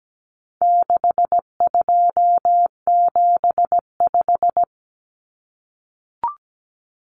Morse code, heard as a keyed tone, about 17 wpm, 700 Hz. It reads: 6275
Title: Common Cut Numbers - 4-Digit 17wpm